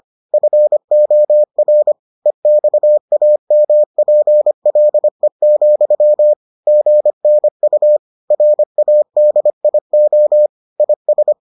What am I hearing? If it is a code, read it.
Morse code, heard as a keyed tone, about 25 wpm, 600 Hz. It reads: FOR EXAMPLE, GNU RADIO IS